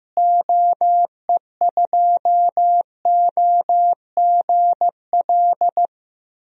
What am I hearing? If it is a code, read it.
Morse code, heard as a keyed tone, about 15 wpm, 700 Hz. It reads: OE2OGL
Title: DXCC-100 Call Signs - 6 Characters - Repeated 3x - 15wpm